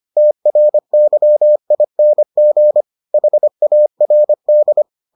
Morse code, heard as a keyed tone, about 25 wpm, 600 Hz. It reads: TRYING HARD